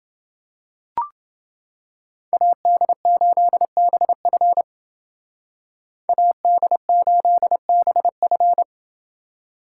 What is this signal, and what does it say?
Morse code, heard as a keyed tone, about 29 wpm, 700 Hz. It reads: AD8BF AD8BF